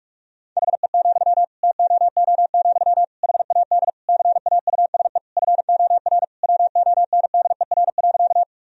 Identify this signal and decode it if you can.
Morse code, heard as a keyed tone, about 46 wpm, 700 Hz. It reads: HE, TOO, HAD CAUSE FOR WONDER.
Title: Sentences from Top 1000 Words - Review Part 3 - 45wpm